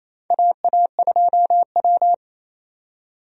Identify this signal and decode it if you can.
Morse code, heard as a keyed tone, about 27 wpm, 700 Hz. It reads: AA2W
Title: USA Call Signs - 4 Characters - 28wpm